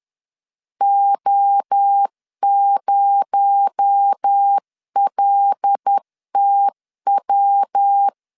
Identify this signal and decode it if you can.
Morse code, heard as a keyed tone, about 10 wpm, 800 Hz. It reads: O0LTW
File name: LEZIONI / 7 GF9Z / 7 G 10wpm 2501